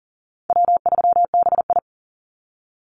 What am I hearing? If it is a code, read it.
Morse code, heard as a keyed tone, about 39 wpm, 700 Hz. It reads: W3BI
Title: Easy Call Signs Repeated 3x - 40wpm